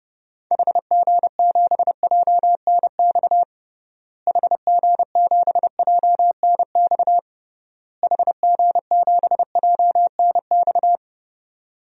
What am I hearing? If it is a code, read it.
Morse code, heard as a keyed tone, about 30 wpm, 700 Hz. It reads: HG7JNX HG7JNX HG7JNX
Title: DXCC-100 Call Signs - 6 Characters - Repeated 3x - 30wpm